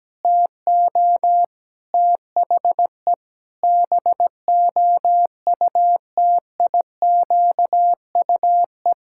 Morse code, heard as a keyed tone, about 17 wpm, 700 Hz. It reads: TO THE BOUTIQUE
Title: Pangram Challenge v2 - 17wpm